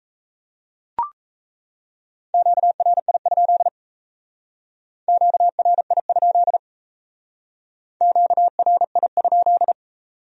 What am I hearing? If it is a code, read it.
Morse code, heard as a keyed tone, about 35 wpm, 700 Hz. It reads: QRI? QRI? QRI?
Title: Speed-Racing Q-Codes - 28wpm